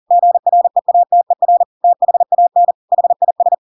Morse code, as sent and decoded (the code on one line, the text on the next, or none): --. .-. .- - . .-. / - .... .- -. / .... .. ...
GREATER THAN HIS